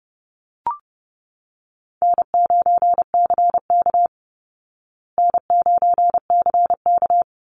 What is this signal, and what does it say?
Morse code, heard as a keyed tone, about 29 wpm, 700 Hz. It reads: N9CK N9CK
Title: Easy Call Signs Repeated 2x - 30wpm